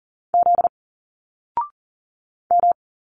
Morse code, heard as a keyed tone, about 40 wpm, 700 Hz. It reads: Z M
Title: Single Letters 40wpm